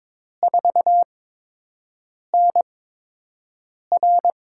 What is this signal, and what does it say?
Morse code, heard as a keyed tone, about 22 wpm, spaced as for 6 wpm, 700 Hz. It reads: 4NR
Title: USA Call Signs - 4 Characters - 8x Character Spacing - 22wpm